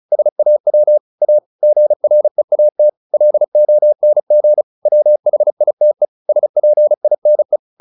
Morse code, heard as a keyed tone, about 34 wpm, 600 Hz. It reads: SAW A GREAT LONG WHITE SPINE
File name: the old man and the sea 2 35wpm